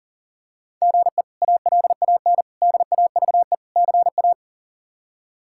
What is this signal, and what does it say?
Morse code, heard as a keyed tone, about 40 wpm, 700 Hz. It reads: GE ALAN DAVE CA
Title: K1USN SST QSO Practice - 40wpm